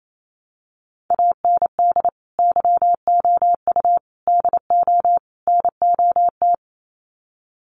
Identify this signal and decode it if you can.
Morse code, heard as a keyed tone, about 28 wpm, 700 Hz. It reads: AND YOU DO NOT
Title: Sets of 4 Words 28wpm